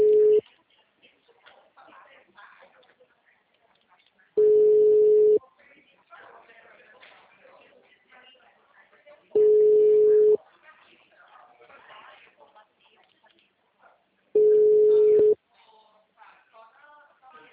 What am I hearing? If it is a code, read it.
Ringback tone (Europe, 425 Hz), with speech.
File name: out- -2007